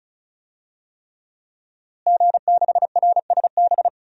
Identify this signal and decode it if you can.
Morse code, heard as a keyed tone, about 34 wpm, 700 Hz. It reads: G6RSB
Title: DXCC-100 Call Signs - 5 Characters - Repeated 3x - 35wpm